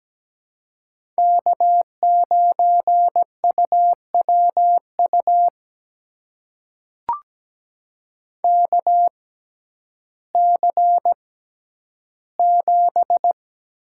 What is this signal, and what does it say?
Morse code, heard as a keyed tone, about 17 wpm, spaced as for 10 wpm, 700 Hz. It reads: K9UWU KC7
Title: USA Call Signs - 5 Characters - 6x Character Spacing - 17wpm